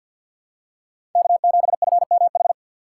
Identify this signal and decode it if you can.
Morse code, heard as a keyed tone, about 54 wpm, 700 Hz. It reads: K7FMH
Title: USA Call Signs - 5 Characters - 6x Character Spacing - 50wpm